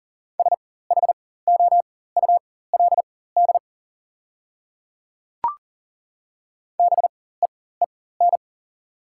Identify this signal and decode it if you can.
Morse code, heard as a keyed tone, about 40 wpm, spaced as for 19 wpm, 700 Hz. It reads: SHOULD BEEN
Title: 4x Character Spacing - Top 300 Words - 40wpm